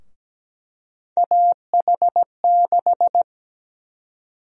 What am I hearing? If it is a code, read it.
Morse code, heard as a keyed tone, about 17 wpm, 700 Hz. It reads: AH6
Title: USA Call Sign Prefixes - Repeated 2x - 17wpm